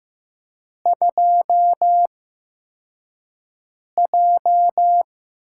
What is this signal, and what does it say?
Morse code, heard as a keyed tone, about 15 wpm, spaced as for 4 wpm, 700 Hz. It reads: 2J
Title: USA Call Signs - 4 Characters - 8x Character Spacing - 15wpm